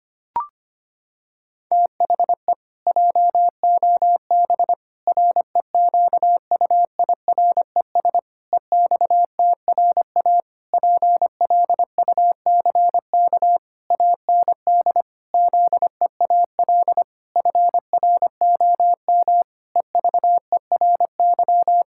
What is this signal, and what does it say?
Morse code, heard as a keyed tone, about 25 wpm, 700 Hz. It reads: THE JOB REQUIRES EXTRA PLUCK AND ZEAL FROM EVERY
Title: Pangram Challenge v2 - 25wpm